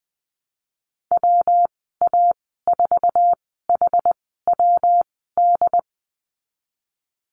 Morse code, heard as a keyed tone, about 20 wpm, spaced as for 15 wpm, 700 Hz. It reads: WA4HWD
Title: 179 Lesson 29g - Slash - Call Signs 20wpm